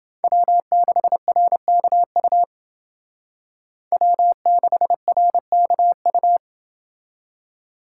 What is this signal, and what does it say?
Morse code, heard as a keyed tone, about 28 wpm, 700 Hz. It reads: W6RKU W6RKU